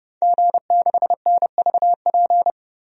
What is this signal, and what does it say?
Morse code, heard as a keyed tone, about 30 wpm, 700 Hz. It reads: G6NVP
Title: DXCC-100 Call Signs - 5 Characters - 30wpm